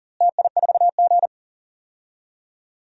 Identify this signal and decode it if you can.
Morse code, heard as a keyed tone, about 40 wpm, 700 Hz. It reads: TI4G
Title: DXCC-100 Call Signs - 4 Characters - Repeated 3x - 40wpm